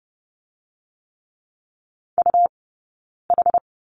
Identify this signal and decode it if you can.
Morse code, heard as a keyed tone, about 30 wpm, spaced as for 9 wpm, 690 Hz. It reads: UH
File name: Lesson 18d - Letter m - Two Characters 30wpm